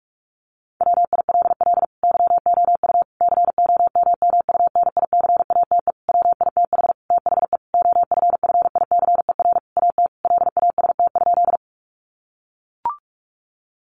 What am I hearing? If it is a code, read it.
Morse code, heard as a keyed tone, about 44 wpm, 700 Hz. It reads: WILL YOU COMMUNICATE WITH THE OFFICER AT LAST?
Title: Sentences from Top 1300 Words 45wpm